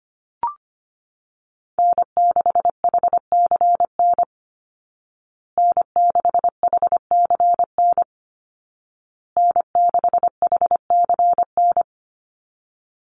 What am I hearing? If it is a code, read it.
Morse code, heard as a keyed tone, about 25 wpm, 700 Hz. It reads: N6HCN N6HCN N6HCN
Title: Call Signs Repeated 2x - 25wpm